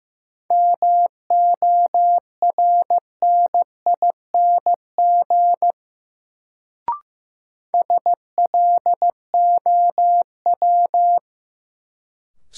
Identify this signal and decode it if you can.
Morse code, heard as a keyed tone, about 15 wpm, 700 Hz. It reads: MORNING SLOW
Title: Sentences from Top 400 Words - New Word Drill - 15wpm